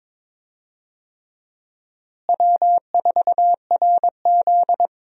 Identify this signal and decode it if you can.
Morse code, heard as a keyed tone, about 21 wpm, 700 Hz. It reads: W4RZ
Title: USA Call Signs - 4 Characters - Repeated 3x - 22wpm